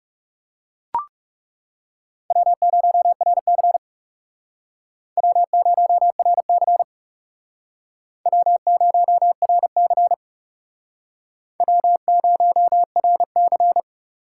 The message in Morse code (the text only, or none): W0RC W0RC W0RC W0RC